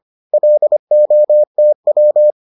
Morse code, heard as a keyed tone, about 25 wpm, 600 Hz. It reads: LOTW